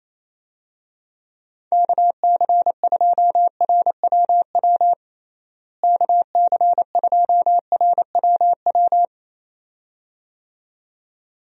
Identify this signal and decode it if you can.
Morse code, heard as a keyed tone, about 28 wpm, 700 Hz. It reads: KC2RWW KC2RWW
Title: USA Call Signs - 6 Characters - Repeated 2x - 28wpm